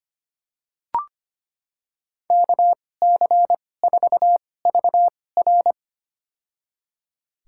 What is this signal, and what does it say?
Morse code, heard as a keyed tone, about 25 wpm, spaced as for 18 wpm, 700 Hz. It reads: KC4VR
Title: USA Call Signs - 5 Characters - 2x Character Spacing - 25wpm